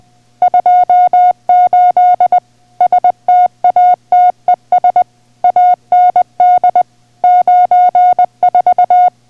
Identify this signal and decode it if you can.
Morse code, heard as a keyed tone, about 20 wpm, 720 Hz. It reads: 28 STATES AND 94